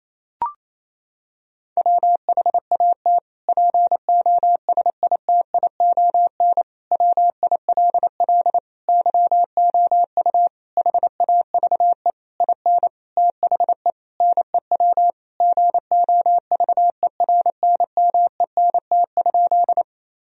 Morse code, heard as a keyed tone, about 28 wpm, 700 Hz. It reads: WHAT POSITION WILL YOU HAVE IN THE NEW GOVERNMENT?
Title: Sentences from Top 1000 Words - Review Part 3 - 28wpm